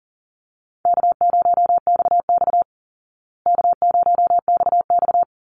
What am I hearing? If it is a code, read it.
Morse code, heard as a keyed tone, about 40 wpm, 700 Hz. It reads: K0XX K0XX
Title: Easy Call Signs Repeated 3x - 40wpm